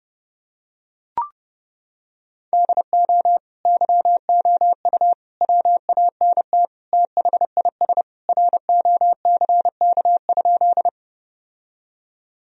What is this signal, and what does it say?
Morse code, heard as a keyed tone, about 32 wpm, 700 Hz. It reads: DO YOU WANT THIS ROCK?
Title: Sentences from Top 1000 Words - Review Part 2 - 30wpm